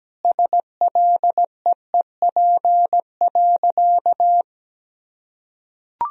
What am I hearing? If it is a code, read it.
Morse code, heard as a keyed tone, about 17 wpm, 700 Hz. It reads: SLEEP.
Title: Sentences from Top 500 Words and Abbreviations 17wpm